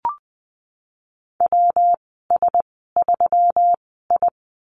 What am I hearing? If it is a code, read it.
Morse code, heard as a keyed tone, about 19 wpm, spaced as for 14 wpm, 700 Hz. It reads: WS3I
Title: 161 Lesson 26g - Letter g - Call Signs 20wpm